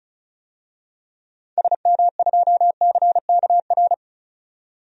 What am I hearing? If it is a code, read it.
Morse code, heard as a keyed tone, about 36 wpm, 700 Hz. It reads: SM2CKR
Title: DXCC-100 Call Signs - 6 Characters - 35wpm